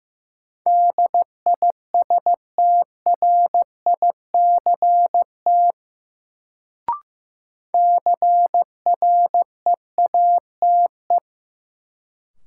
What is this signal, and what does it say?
Morse code, heard as a keyed tone, about 15 wpm, 700 Hz. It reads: DISTRICT CREATE
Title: Sentences from Top 1600 Words - New Word Drill Alt - 15wpm